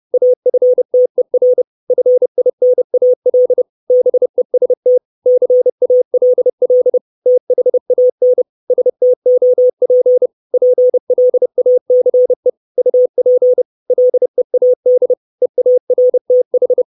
Morse code, heard as a keyed tone, about 30 wpm, 500 Hz. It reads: AFTER FINAL BEST CALL THAN STOP PLACE UP LEAD EARTH
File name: 632 Words 30WPM